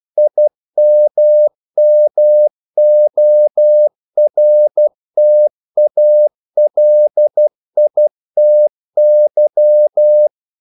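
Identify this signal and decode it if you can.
Morse code, heard as a keyed tone, about 12 wpm, 600 Hz. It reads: IMMORTALITY
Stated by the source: Morse practice sending English quotations.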